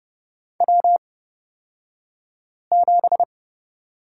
Morse code, heard as a keyed tone, about 30 wpm, spaced as for 5 wpm, 700 Hz. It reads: W7
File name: Lesson 36d - BT - Two Characters 30wpm